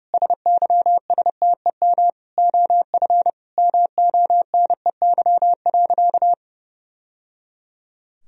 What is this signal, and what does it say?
Morse code, heard as a keyed tone, about 30 wpm, 700 Hz. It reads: SYSTEM OF MONEY.
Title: Sentences from Top 1000 Words - Review Part 4 - 30wpm